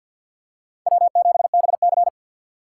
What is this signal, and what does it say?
Morse code, heard as a keyed tone, about 51 wpm, 700 Hz. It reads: W7BC
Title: USA Call Signs - 4 Characters - 2x Character Spacing - 50wpm